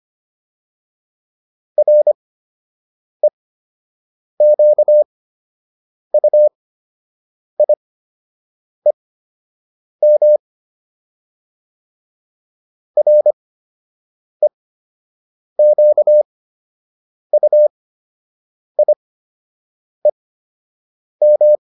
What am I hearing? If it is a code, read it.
Morse code, heard as a keyed tone, about 25 wpm, spaced as for 7 wpm, 600 Hz. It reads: REQUIEM REQUIEM